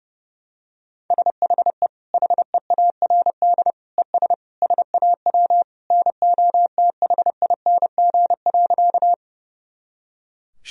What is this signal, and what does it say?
Morse code, heard as a keyed tone, about 30 wpm, 700 Hz. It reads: SHE HEARD ES SAW NOTHING.
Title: Sentences from Top 500 Words and Abbreviations 30wpm